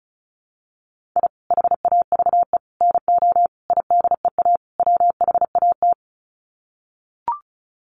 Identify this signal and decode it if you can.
Morse code, heard as a keyed tone, about 34 wpm, 700 Hz. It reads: I HAVE NO IDEA WHAT
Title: Sets of 5 Words Encore 35wpm